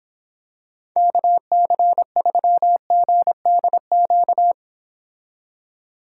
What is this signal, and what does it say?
Morse code, heard as a keyed tone, about 26 wpm, 700 Hz. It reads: KC3GDQ